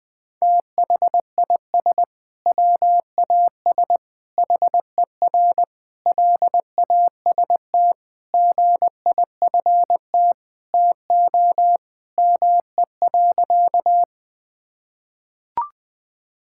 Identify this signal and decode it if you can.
Morse code, heard as a keyed tone, about 21 wpm, 700 Hz. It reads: THIS WAS HER LAST GIFT TO ME.
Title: Sentences from Top 1000 Words - Review Part 6 - 20wpm